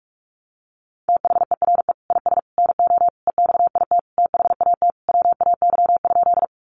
Morse code, heard as a keyed tone, about 44 wpm, 700 Hz. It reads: THERE IS NO EXIT THAT WAY?
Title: Sentences from Top 1100 Words 45wpm